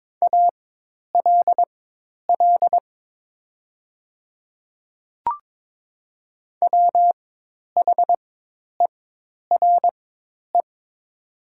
Morse code, heard as a keyed tone, about 22 wpm, spaced as for 10 wpm, 700 Hz. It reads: ALL WHERE
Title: 4x Character Spacing - Top 200 Words - 22wpm